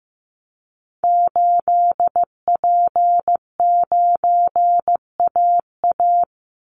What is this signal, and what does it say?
Morse code, heard as a keyed tone, about 15 wpm, 700 Hz. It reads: 8P9AA
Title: Call Signs Repeated 3x - 15wpm